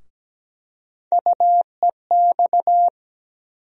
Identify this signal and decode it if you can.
Morse code, heard as a keyed tone, about 17 wpm, 700 Hz. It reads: UEX
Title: USA Call Sign Postfixes - 3 Letters 17wpm